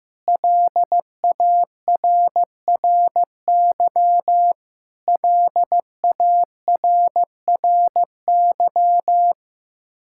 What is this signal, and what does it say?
Morse code, heard as a keyed tone, about 15 wpm, 700 Hz. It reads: LARRY LARRY
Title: Practice QSO - Exchange 1 - Beginning - 15wpm